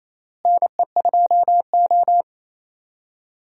Morse code, heard as a keyed tone, about 28 wpm, 700 Hz. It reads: NE2O
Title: USA Call Signs - 4 Characters - Repeated 3x - 28wpm